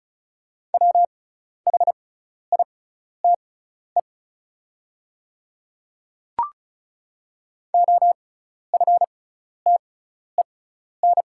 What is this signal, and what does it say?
Morse code, heard as a keyed tone, about 36 wpm, spaced as for 12 wpm, 700 Hz. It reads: WHITE OFTEN
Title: 6x Character Spacing - Top 500 Words - 35wpm